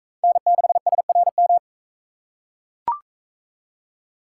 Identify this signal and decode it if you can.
Morse code, heard as a keyed tone, about 40 wpm, 700 Hz. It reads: N6SRM